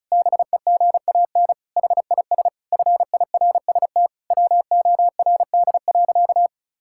Morse code, heard as a keyed tone, about 36 wpm, 700 Hz. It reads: BEGAN HIS FIRST WORD.
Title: Pangram Challenge v2 - 35wpm